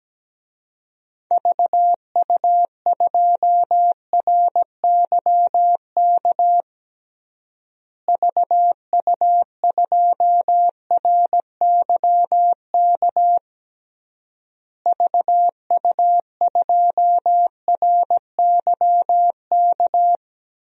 Morse code, heard as a keyed tone, about 17 wpm, 700 Hz. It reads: VU2RYK VU2RYK VU2RYK
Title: DXCC-100 Call Signs - 6 Characters - Repeated 3x - 17wpm